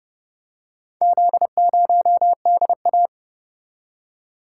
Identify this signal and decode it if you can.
Morse code, heard as a keyed tone, about 29 wpm, 700 Hz. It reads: Z0DA